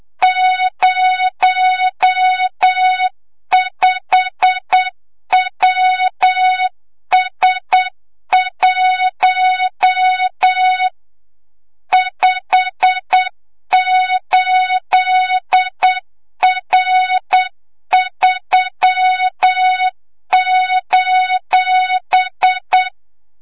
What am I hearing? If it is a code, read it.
Morse code, heard as a keyed tone, about 8 wpm, 3700 Hz, with noise. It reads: 05WS1 58R3:
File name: Morze-HA2MN / hang / 25 GYAKORLAT